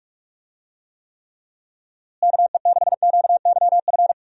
Morse code, heard as a keyed tone, about 46 wpm, 700 Hz. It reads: KE6QYF
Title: USA Call Signs - 6 Characters - Repeated 3x - 45wpm